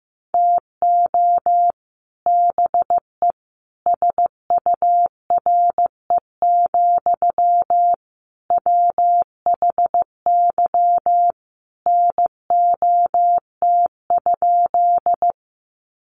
Morse code, heard as a keyed tone, about 15 wpm, 700 Hz. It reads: TO BE SURE, WHY NOT?
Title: Sentences from Top 600 Words 15wpm